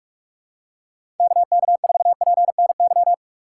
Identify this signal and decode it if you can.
Morse code, heard as a keyed tone, about 47 wpm, 700 Hz. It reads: KK4PNY
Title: USA Call Signs - 6 Characters - 6x Character Spacing - 45wpm